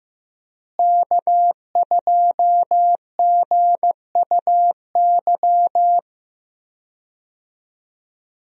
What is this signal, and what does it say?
Morse code, heard as a keyed tone, about 15 wpm, 700 Hz. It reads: K2GUY
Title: USA Call Signs - 5 Characters - Repeated 3x - 15wpm